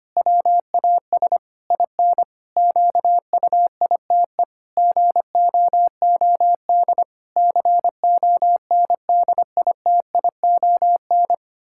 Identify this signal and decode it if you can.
Morse code, heard as a keyed tone, about 25 wpm, 700 Hz. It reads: WAS IN QUITE GOOD CONDITION
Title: Pangram Challenge v2 - 25wpm